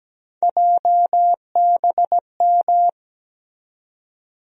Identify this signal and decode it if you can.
Morse code, heard as a keyed tone, about 17 wpm, 700 Hz. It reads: JBM
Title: USA Call Sign Postfixes - 3 Letters 17wpm